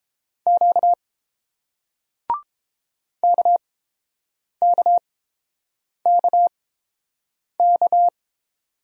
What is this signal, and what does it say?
Morse code, heard as a keyed tone, about 30 wpm, spaced as for 7 wpm, 700 Hz. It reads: Q KKKK